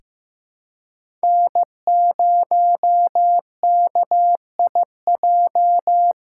Morse code, heard as a keyed tone, about 15 wpm, 700 Hz. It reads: N0KIJ
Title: USA Callsign DE Callsign - 5 Characters - 15wpm